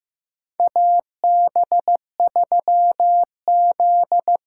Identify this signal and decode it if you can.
Morse code, heard as a keyed tone, about 15 wpm, 700 Hz. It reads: AB3Z
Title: USA Call Signs - 5 Characters - 6x Character Spacing - 15wpm